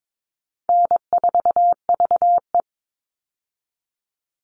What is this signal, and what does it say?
Morse code, heard as a keyed tone, about 22 wpm, 700 Hz. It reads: N4VE
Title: All Cut Numbers - 4-Digit 22wpm